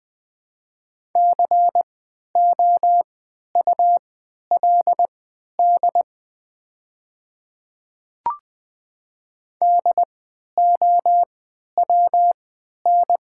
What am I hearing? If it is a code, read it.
Morse code, heard as a keyed tone, about 20 wpm, spaced as for 11 wpm, 700 Hz. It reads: COULD DOWN
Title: 3x Character Spacing - Top 100 Words - 20wpm